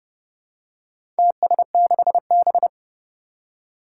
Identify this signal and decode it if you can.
Morse code, heard as a keyed tone, about 30 wpm, 700 Hz. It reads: TS6B
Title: Four Challenging Letter-Number Combinations 30wpm